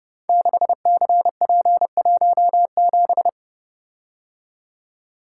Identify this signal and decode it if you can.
Morse code, heard as a keyed tone, about 30 wpm, 700 Hz. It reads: 6CP17
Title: Code Groups - 5 Letter:Numbers - 30wpm